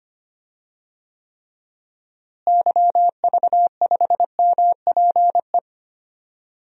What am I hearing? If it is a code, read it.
Morse code, heard as a keyed tone, about 25 wpm, 700 Hz. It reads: YV5MPE
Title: DXCC-100 Call Signs - 6 Characters - Repeated 3x - 25wpm